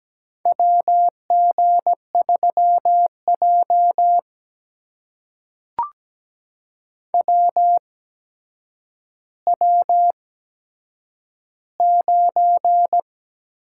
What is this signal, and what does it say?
Morse code, heard as a keyed tone, about 17 wpm, spaced as for 8 wpm, 700 Hz. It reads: WG3J WW9
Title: USA Call Signs - 4 Characters - 8x Character Spacing - 17wpm